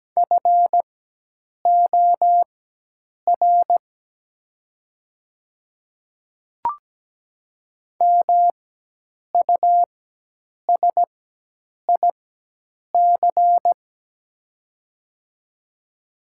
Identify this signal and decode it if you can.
Morse code, heard as a keyed tone, about 17 wpm, spaced as for 8 wpm, 690 Hz. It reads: FOR MUSIC
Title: 4x Character Spacing - Top 300 Words - 17wpm